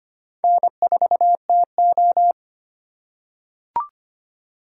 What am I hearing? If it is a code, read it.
Morse code, heard as a keyed tone, about 25 wpm, 700 Hz. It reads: N4TO
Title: USA Call Signs - 4x Character Spacing - 25wpm